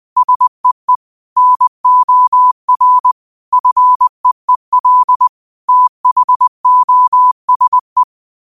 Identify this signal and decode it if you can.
Morse code, heard as a keyed tone, about 20 wpm, 1000 Hz. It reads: SEE NOR FEEL THOSE